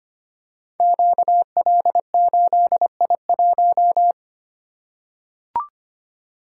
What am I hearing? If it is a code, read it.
Morse code, heard as a keyed tone, about 25 wpm, 700 Hz. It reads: QL8I1
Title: Code Groups - 5 Letter:Numbers - 25wpm